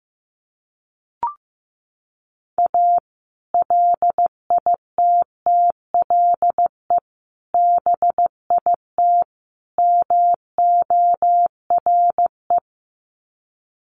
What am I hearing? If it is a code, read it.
Morse code, heard as a keyed tone, about 15 wpm, 700 Hz. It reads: A LITTLE BIT MORE
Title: Sets of 4 Words 15wpm